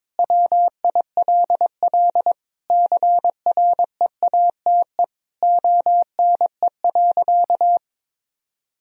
Morse code, heard as a keyed tone, about 22 wpm, 700 Hz. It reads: WILL CREATE ONE.
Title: Sentences from Top 1000 Words - Review Part 3 - 22wpm